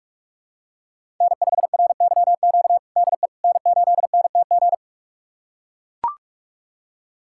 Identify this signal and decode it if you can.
Morse code, heard as a keyed tone, about 41 wpm, 700 Hz. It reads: N5RYQ DE N8NTG
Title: USA Callsign DE Callsign - 5 Characters - 45wpm